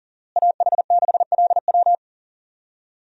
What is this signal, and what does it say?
Morse code, heard as a keyed tone, about 40 wpm, 700 Hz. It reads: AH6LW
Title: USA Call Signs - 5 Characters - Repeated 2x - 40wpm